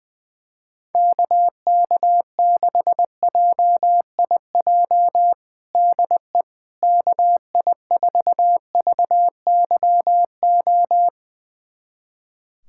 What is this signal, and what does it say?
Morse code, heard as a keyed tone, about 21 wpm, 700 Hz. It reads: KK6JIJ DE KI4VYO
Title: USA Callsign DE Callsign - 6 Characters - 20wpm